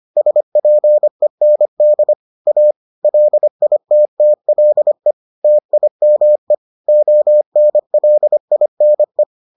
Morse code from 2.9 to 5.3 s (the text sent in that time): LITTLE